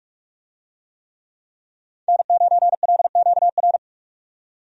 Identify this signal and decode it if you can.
Morse code, heard as a keyed tone, about 46 wpm, 700 Hz. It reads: N9LQR
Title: USA Call Signs - 5 Characters - 45wpm